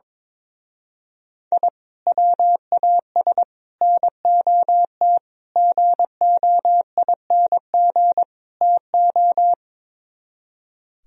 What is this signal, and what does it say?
Morse code, heard as a keyed tone, about 22 wpm, 700 Hz. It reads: I WAS NOT GOING TO
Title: Speed-Racing - Sets of 5 Words 22wpm